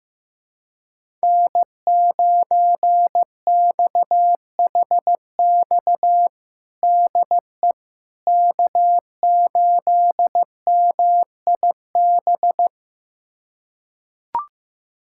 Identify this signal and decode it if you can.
Morse code, heard as a keyed tone, about 15 wpm, 700 Hz. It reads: N9XHX DE K8MIB
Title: USA Callsign DE Callsign - 5 Characters - 15wpm